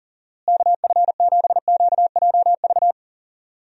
Morse code, heard as a keyed tone, about 40 wpm, 700 Hz. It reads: KF7QJV
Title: USA Call Signs - 6 Characters - Repeated 3x - 40wpm